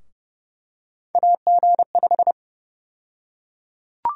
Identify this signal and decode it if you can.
Morse code, heard as a keyed tone, about 30 wpm, 700 Hz. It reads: AG5
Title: USA Call Sign Prefixes - Repeated 2x - 30wpm